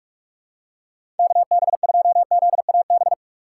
Morse code, heard as a keyed tone, about 46 wpm, 700 Hz. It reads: KB2ZAB
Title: USA Call Signs - 6 Characters - 4x Character Spacing - 45wpm